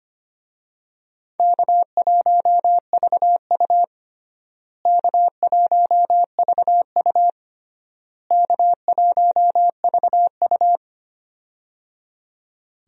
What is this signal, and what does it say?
Morse code, heard as a keyed tone, about 25 wpm, 700 Hz. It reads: K1VU K1VU K1VU
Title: USA Call Signs - 4 Characters - Repeated 3x - 25wpm